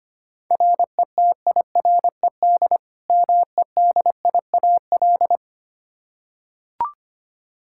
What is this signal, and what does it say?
Morse code, heard as a keyed tone, about 25 wpm, 700 Hz. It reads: RETIRED MEDIAL
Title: Practice QSO - Exchange 3 - More Info - 25wpm